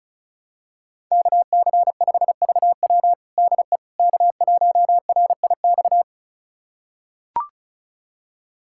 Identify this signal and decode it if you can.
Morse code, heard as a keyed tone, about 36 wpm, 700 Hz. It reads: KC5VW DE K1RIX
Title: USA Callsign DE Callsign 35wpm